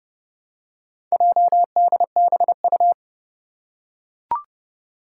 Morse code, heard as a keyed tone, about 30 wpm, 700 Hz. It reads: JDBU
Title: Four Challenging Letter-Number Combinations 30wpm